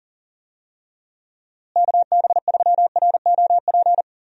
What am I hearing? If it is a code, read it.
Morse code, heard as a keyed tone, about 40 wpm, 700 Hz. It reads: KB3ROP
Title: USA Call Signs - 6 Characters - Repeated 3x - 40wpm